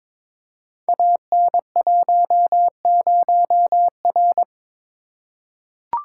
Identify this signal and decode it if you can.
Morse code, heard as a keyed tone, about 22 wpm, 700 Hz. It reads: AN10R
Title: Code Groups - 5 Letter:Numbers - 22wpm